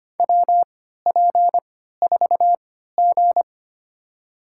Morse code, heard as a keyed tone, about 25 wpm, spaced as for 14 wpm, 700 Hz. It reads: WP4G
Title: USA Call Signs - 4 Characters - 3x Character Spacing - 25wpm